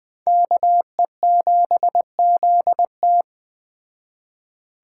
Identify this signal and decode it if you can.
Morse code, heard as a keyed tone, about 20 wpm, 700 Hz. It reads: KE7ZT
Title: Speed Racing - USA Call Signs 17wpm